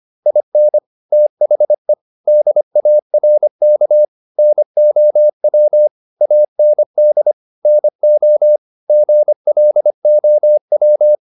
Morse code, heard as a keyed tone, about 25 wpm, 600 Hz. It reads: IN THE DARK NOW AND NO GLOW